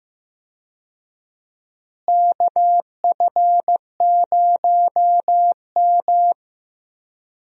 Morse code, heard as a keyed tone, about 15 wpm, 700 Hz. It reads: KF0M
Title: USA Call Signs - 4 Characters - Repeated 2x - 15wpm